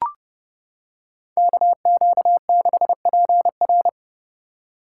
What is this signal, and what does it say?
Morse code, heard as a keyed tone, about 29 wpm, 700 Hz. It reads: KQ6PR